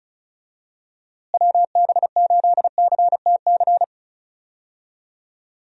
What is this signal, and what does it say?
Morse code, heard as a keyed tone, about 34 wpm, 700 Hz. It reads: WB8CTC
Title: USA Call Signs - 6 Characters - Repeated 3x - 35wpm